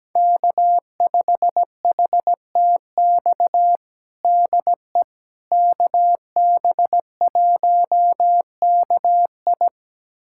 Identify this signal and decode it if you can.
Morse code, heard as a keyed tone, about 17 wpm, 700 Hz. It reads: K5HTX DE KB1KI
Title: USA Callsign DE Callsign - 5 Characters - 17wpm